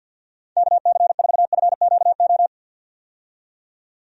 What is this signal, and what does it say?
Morse code, heard as a keyed tone, about 52 wpm, 700 Hz. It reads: KC4FQO
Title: USA Call Signs - 6 Characters - 50wpm